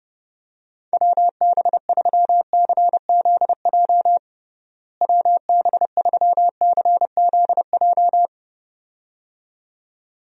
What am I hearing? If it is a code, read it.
Morse code, heard as a keyed tone, about 32 wpm, 700 Hz. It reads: WB3CZJ WB3CZJ
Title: USA Call Signs - 6 Characters - Repeated 2x - 30wpm